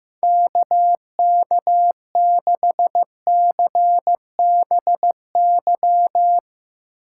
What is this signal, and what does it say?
Morse code, heard as a keyed tone, about 15 wpm, 700 Hz. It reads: KK6CBY
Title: Speed Racing - USA 2x3 Call Signs 15wpm